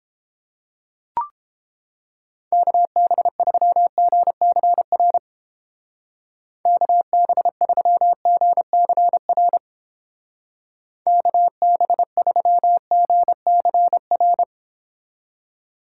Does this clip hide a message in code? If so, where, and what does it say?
Yes: Morse code from 2.5 to 14.4 s, heard as a keyed tone, about 30 wpm, 700 Hz. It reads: KB3GCR KB3GCR KB3GCR